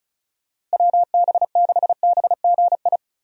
Morse code, heard as a keyed tone, about 34 wpm, 700 Hz. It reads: WB6BGI
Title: USA Call Signs - 6 Characters - Repeated 2x - 35wpm